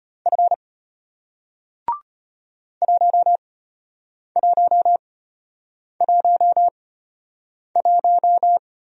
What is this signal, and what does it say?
Morse code, heard as a keyed tone, about 33 wpm, spaced as for 8 wpm, 700 Hz. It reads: F 1111